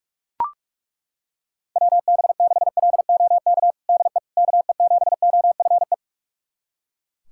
Morse code, heard as a keyed tone, about 41 wpm, 690 Hz. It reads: WB6LOK DE KE7OFE
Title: USA Callsign DE Callsign - 6 Characters - 45wpm